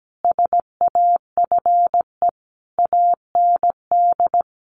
Morse code, heard as a keyed tone, about 17 wpm, 700 Hz. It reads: SAFE AND
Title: Sentences from Top 900 Words 17wpm